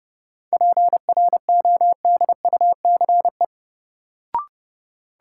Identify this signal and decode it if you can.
Morse code, heard as a keyed tone, about 31 wpm, 700 Hz. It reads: PRODUCE
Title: Sentences from Top 1000 Words - Review Part 3 - New Word Drill - 30wpm